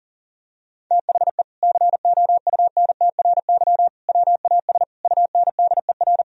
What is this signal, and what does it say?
Morse code, heard as a keyed tone, about 40 wpm, 700 Hz. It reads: THE COUNTRY WAS UNDER
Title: Sentences from Top 1000 Words - Review Part 1 - 40wpm